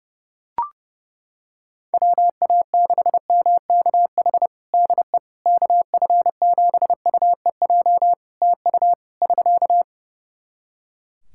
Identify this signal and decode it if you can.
Morse code, heard as a keyed tone, about 30 wpm, 700 Hz. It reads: WA6MKH DE KF7UEJ TU <SK>
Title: K1USN SST QSO Practice 30wpm